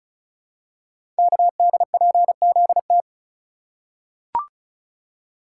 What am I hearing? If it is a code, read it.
Morse code, heard as a keyed tone, about 36 wpm, 700 Hz. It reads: KDPZT
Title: Code Groups - 5 Letters - 35wpm